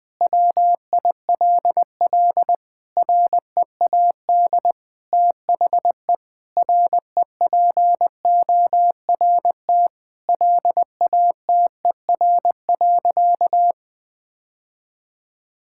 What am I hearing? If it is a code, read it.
Morse code, heard as a keyed tone, about 20 wpm, 700 Hz. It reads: WILL READ THE REPORT LATER.
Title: Sentences from Top 1000 Words - Review Part 6 - 20wpm